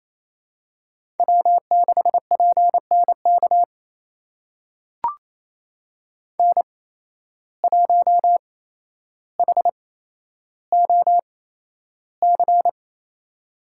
Morse code, heard as a keyed tone, about 27 wpm, spaced as for 12 wpm, 700 Hz. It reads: W6PNK N1HOC
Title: USA Call Signs - 5 Characters - 8x Character Spacing - 28wpm